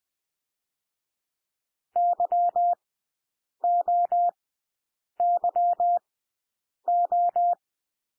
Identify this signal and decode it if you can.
Morse code, heard as a keyed tone, about 20 wpm, spaced as for 8 wpm, 700 Hz. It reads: YOYO
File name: cwmp3 / W1 SOUND WORDS 20 wpm